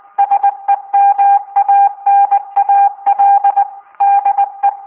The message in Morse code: SEMANAL DE